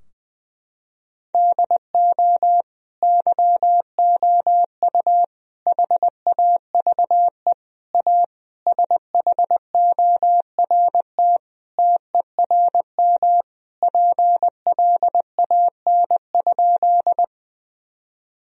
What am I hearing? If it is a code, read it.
Morse code, heard as a keyed tone, about 19 wpm, 700 Hz. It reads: DO YOU HAVE A SHORT TERM PLAN?
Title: Sentences from Top 1000 Words - Review Part 4 - 20wpm